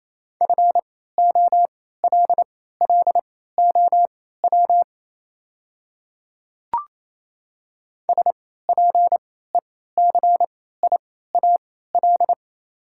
Morse code, heard as a keyed tone, about 28 wpm, spaced as for 16 wpm, 700 Hz. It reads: FOLLOW SPECIAL
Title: 3x Character Spacing - Top 500 Words - 28wpm